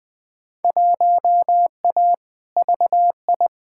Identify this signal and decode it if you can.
Morse code, heard as a keyed tone, about 20 wpm, 700 Hz. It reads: 1A VI
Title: Field Day Exchanges Encore - 20wpm